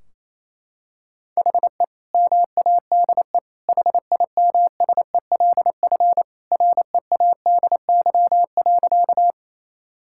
Morse code, heard as a keyed tone, about 28 wpm, 700 Hz. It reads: HE MADE HIMSELF READY.
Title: Sentences from Top 1600 Words 28wpm